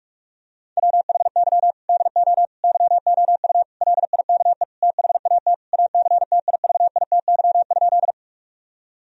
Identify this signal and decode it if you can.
Morse code, heard as a keyed tone, about 45 wpm, 700 Hz. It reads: WHY DO YOU LIKE THAT ACTIVITY?
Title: Sentences from Top 1000 Words - Review Part 4 - 45wpm